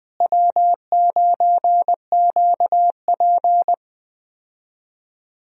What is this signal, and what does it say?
Morse code, heard as a keyed tone, about 20 wpm, 700 Hz. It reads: W9QP
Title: USA Call Signs - 4 Characters - Repeated 3x - 20wpm